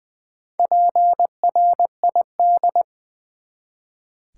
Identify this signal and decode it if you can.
Morse code, heard as a keyed tone, about 20 wpm, 700 Hz. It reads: PRID
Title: Build A Word - 1 Letter At A Time - 3 to 5-Letter Words - 20wpm